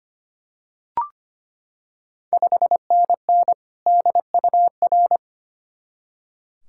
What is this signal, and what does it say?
Morse code, heard as a keyed tone, about 25 wpm, 690 Hz. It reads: 5NN DUR
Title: Ontario QSO Party Exchange 25wpm